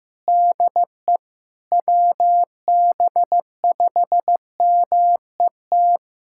Morse code, heard as a keyed tone, about 15 wpm, 700 Hz. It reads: DE WB5MET
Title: USA Callsign DE Callsign - 6 Characters - 15wpm